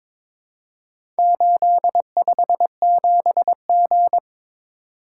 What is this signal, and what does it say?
Morse code, heard as a keyed tone, about 22 wpm, 700 Hz. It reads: 857G
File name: practice / 22wpm / Four Letter-Number Combinations - 22wpm